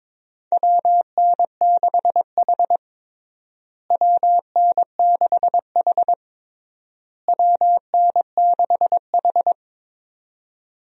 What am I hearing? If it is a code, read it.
Morse code, heard as a keyed tone, about 22 wpm, 700 Hz. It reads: WN6H WN6H WN6H
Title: USA Call Signs - 4 Characters - Repeated 3x - 22wpm